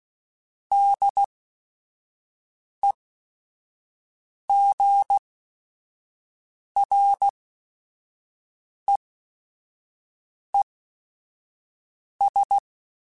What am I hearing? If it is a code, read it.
Morse code, heard as a keyed tone, about 16 wpm, spaced as for 5 wpm, 800 Hz. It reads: DEGREES
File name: Weather Sets / Weather Sets-mp3s / Weather 5 22 @5